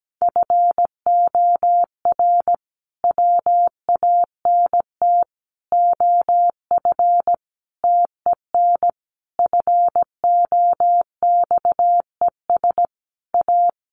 Morse code, heard as a keyed tone, about 17 wpm, 700 Hz. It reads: FOR WANT OF TEN FOXES A
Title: A Sound Of Thunder 17wpm